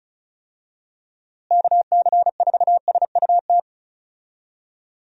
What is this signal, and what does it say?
Morse code, heard as a keyed tone, about 34 wpm, 700 Hz. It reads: KC4SUT
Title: USA Call Signs - 6 Characters - Repeated 2x - 35wpm